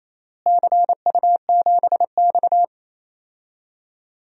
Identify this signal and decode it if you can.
Morse code, heard as a keyed tone, about 28 wpm, 700 Hz. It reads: CU7X
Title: DXCC-100 Call Signs - 4 Characters - Repeated 2x - 28wpm